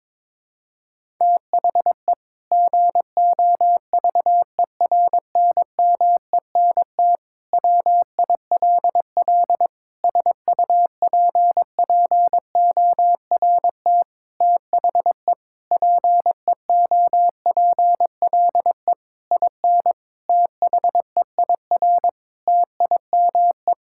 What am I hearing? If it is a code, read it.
Morse code, heard as a keyed tone, about 22 wpm, 700 Hz. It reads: THE GOVERNMENT WILL SUPPORT THE PEOPLE IN THEIR TIME
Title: Sentences from Top 1000 Words - Review Part 5 - 22wpm